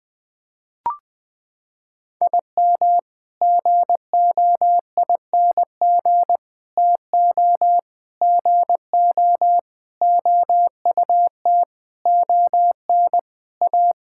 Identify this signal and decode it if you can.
Morse code, heard as a keyed tone, about 21 wpm, 700 Hz. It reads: IM GOING TO GO OUT ON A
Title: Sets of 8 Words - 20wpm